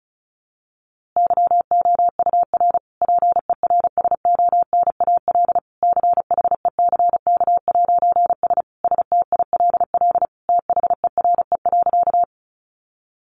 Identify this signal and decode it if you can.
Morse code, heard as a keyed tone, about 34 wpm, 700 Hz. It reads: YOUR PERSONAL CHECK'S STILL THERE.
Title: A Sound Of Thunder 35wpm